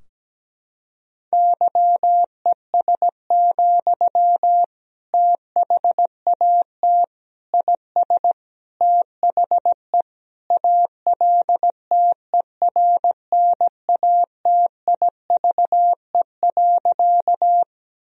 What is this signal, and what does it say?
Morse code, heard as a keyed tone, about 17 wpm, 700 Hz. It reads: YES, THAT IS THE ALTERNATIVE.
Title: Sentences from Top 1000 Words - Review Part 4 - 17wpm